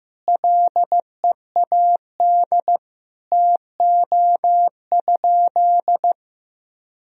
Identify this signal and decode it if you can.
Morse code, heard as a keyed tone, about 15 wpm, 700 Hz. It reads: LEAD TO?
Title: Sentences from Top 1000 Words - Review Part 2 - 15wpm